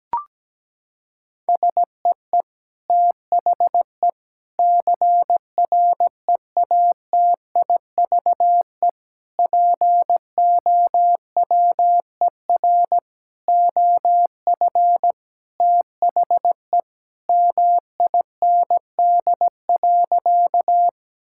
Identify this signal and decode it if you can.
Morse code, heard as a keyed tone, about 17 wpm, 700 Hz. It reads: SEE THE CREATIVE POWER OF THE MIND.
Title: Sentences from Top 1000 Words - Review Part 5 - 17wpm